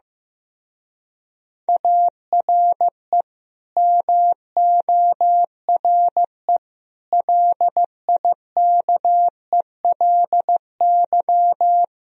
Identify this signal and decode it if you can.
Morse code, heard as a keyed tone, about 15 wpm, 700 Hz. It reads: ARE MORE LIKELY
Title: Speed-Racing - Sets of 5 Words 15wpm